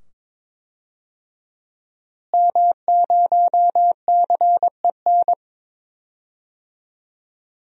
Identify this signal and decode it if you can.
Morse code, heard as a keyed tone, about 22 wpm, 700 Hz. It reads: M0CEN
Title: DXCC-100 Call Signs - 5 Characters - Repeated 2x - 22wpm